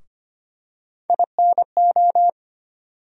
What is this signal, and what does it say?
Morse code, heard as a keyed tone, about 25 wpm, 700 Hz. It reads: INO
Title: USA Call Sign Postfixes - 3 Letters 25wpm